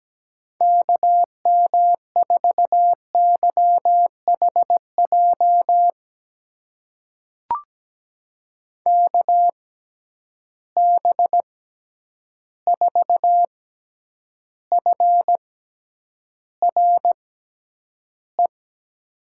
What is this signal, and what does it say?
Morse code, heard as a keyed tone, about 17 wpm, spaced as for 9 wpm, 700 Hz. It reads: KM4YHJ KB4FRE
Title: USA Call Signs - 6 Characters - 6x Character Spacing - 17wpm